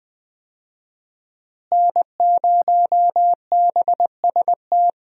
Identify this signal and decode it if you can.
Morse code, heard as a keyed tone, about 20 wpm, 700 Hz. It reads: N0BST
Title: USA Call Signs - 5 Characters - Repeated 2x - 20wpm